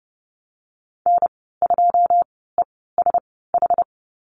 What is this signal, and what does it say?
Morse code, heard as a keyed tone, about 30 wpm, spaced as for 17 wpm, 700 Hz. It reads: N2ESH
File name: Lesson 14g - Number 2 - Call Signs 30wpm